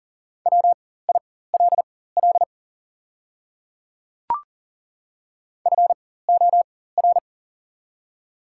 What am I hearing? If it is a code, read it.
Morse code, heard as a keyed tone, about 40 wpm, spaced as for 19 wpm, 700 Hz. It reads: WILL FOR
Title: 4x Character Spacing - Top 200 Words - 40wpm